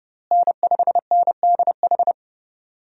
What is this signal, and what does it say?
Morse code, heard as a keyed tone, about 31 wpm, 700 Hz. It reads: N5NDH
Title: USA Call Signs - 5 Characters - Repeated 2x - 30wpm